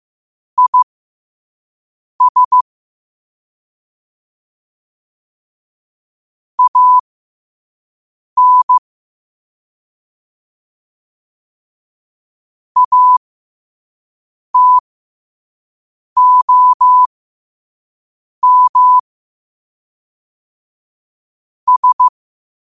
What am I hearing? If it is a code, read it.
Morse code, heard as a keyed tone, about 15 wpm, spaced as for 5 wpm, 1000 Hz. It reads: IS AN ATOM S
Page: New quotes every day in morse code at 5 Words per minute.